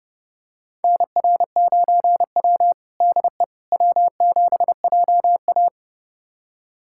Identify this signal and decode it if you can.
Morse code, heard as a keyed tone, about 30 wpm, 700 Hz. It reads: NR9W DE W7JA
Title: USA Callsign DE Callsign - 4 Characters - 30wpm